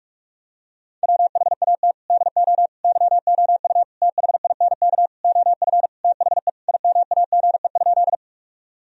Morse code, heard as a keyed tone, about 41 wpm, 700 Hz. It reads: WHAT DO YOU THINK OF THE IMAGE?
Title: Sentences from Top 1000 Words - Review Part 4 - 45wpm